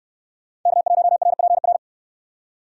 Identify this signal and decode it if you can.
Morse code, heard as a keyed tone, about 68 wpm, 700 Hz. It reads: K1RPR